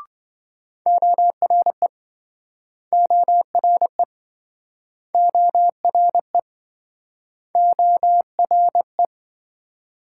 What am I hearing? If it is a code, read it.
Morse code, heard as a keyed tone, about 27 wpm, 700 Hz. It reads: ORE ORE ORE ORE